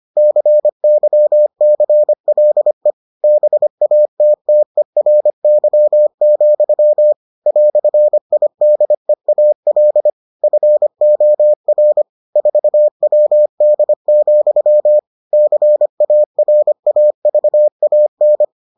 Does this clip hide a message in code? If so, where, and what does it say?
Yes: Morse code from 0.2 to 18.4 s, heard as a keyed tone, about 25 wpm, 600 Hz. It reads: CYCLE BATTERY, "IDEAL FOR 4WD, CARAVAN